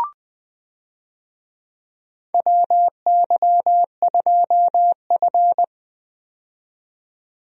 Morse code, heard as a keyed tone, about 20 wpm, 700 Hz. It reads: WY2F
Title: USA Call Signs - 4 Characters - 20wpm